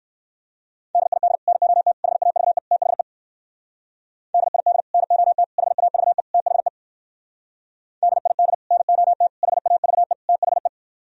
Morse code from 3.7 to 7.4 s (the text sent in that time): DID NOT HAVE THE